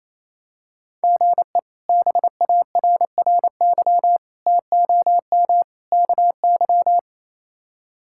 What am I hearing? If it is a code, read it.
Morse code, heard as a keyed tone, about 28 wpm, 700 Hz. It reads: GE BARRY TOM KY
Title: K1USN SST QSO Practice - 28wpm